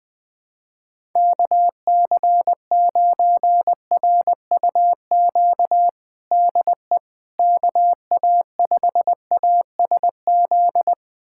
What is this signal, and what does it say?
Morse code, heard as a keyed tone, about 20 wpm, 700 Hz. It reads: KC9RUQ DE KA5ASZ
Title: USA Callsign DE Callsign - 6 Characters - 20wpm